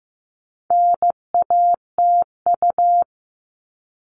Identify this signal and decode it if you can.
Morse code, heard as a keyed tone, about 15 wpm, 700 Hz. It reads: NATU
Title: All Cut Numbers - 4-Digit 15wpm